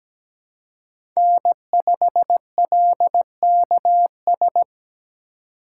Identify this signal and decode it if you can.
Morse code, heard as a keyed tone, about 17 wpm, 700 Hz. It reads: N5LKS
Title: USA Call Signs - 5 Characters - 2x Character Spacing - 17wpm